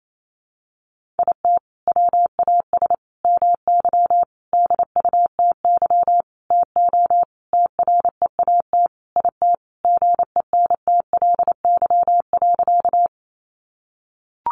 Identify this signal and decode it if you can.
Morse code, heard as a keyed tone, about 28 wpm, 700 Hz. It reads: IT WAS MY DUTY TO TREAT IT GENTLY.
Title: Sentences from Top 1500 Words 28wpm